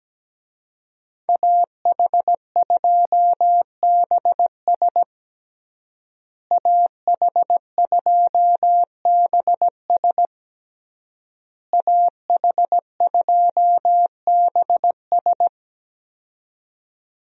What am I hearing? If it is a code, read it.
Morse code, heard as a keyed tone, about 17 wpm, 700 Hz. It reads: AH2BS AH2BS AH2BS
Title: USA Call Signs - 5 Characters - Repeated 3x - 17wpm